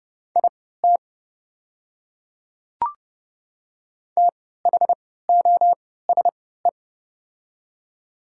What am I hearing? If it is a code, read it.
Morse code, heard as a keyed tone, about 30 wpm, spaced as for 17 wpm, 700 Hz. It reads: IT THOSE
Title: 3x Character Spacing - Top 300 Words - 30wpm